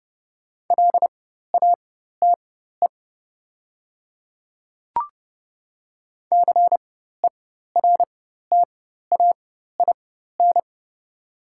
Morse code, heard as a keyed tone, about 29 wpm, spaced as for 14 wpm, 690 Hz. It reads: LATE CERTAIN
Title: 4x Character Spacing - Top 500 Words - 30wpm